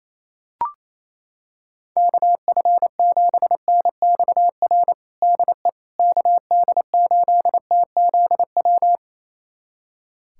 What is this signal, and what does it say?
Morse code, heard as a keyed tone, about 27 wpm, 700 Hz. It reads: KF7NXR DE KD8TZW
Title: USA Callsign DE Callsign - 6 Characters - 28wpm